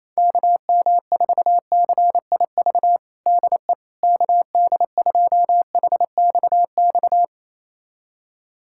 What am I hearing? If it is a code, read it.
Morse code, heard as a keyed tone, about 28 wpm, 700 Hz. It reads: KM4CIV DE KD2HXX
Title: USA Callsign DE Callsign 28wpm